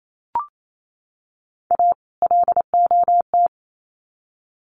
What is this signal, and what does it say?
Morse code, heard as a keyed tone, about 28 wpm, 700 Hz. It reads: A LOT
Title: Sets of 2 Words 28wpm